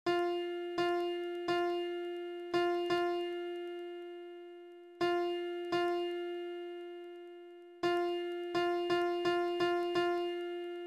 There is no code.